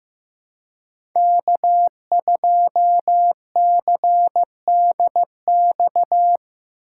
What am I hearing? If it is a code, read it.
Morse code, heard as a keyed tone, about 15 wpm, 700 Hz. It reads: K2CDX
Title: K1USN SST QSO Practice - 15wpm